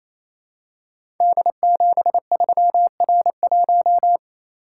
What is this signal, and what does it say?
Morse code, heard as a keyed tone, about 27 wpm, 700 Hz. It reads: D73R1
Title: Code Groups - 5 Letter:Numbers - 28wpm